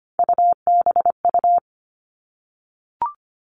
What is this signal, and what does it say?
Morse code, heard as a keyed tone, about 25 wpm, 700 Hz. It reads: UBU
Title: All Cut Numbers - 3-Digit 25wpm